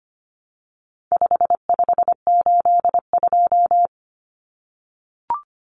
Morse code, heard as a keyed tone, about 25 wpm, 700 Hz. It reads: 5582
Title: Numbers - 4-Digit 25wpm